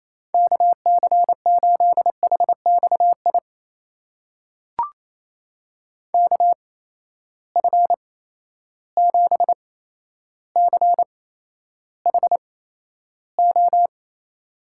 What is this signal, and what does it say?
Morse code, heard as a keyed tone, about 28 wpm, spaced as for 12 wpm, 700 Hz. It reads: KC8HXI KF7CHO
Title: USA Call Signs - 6 Characters - 8x Character Spacing - 28wpm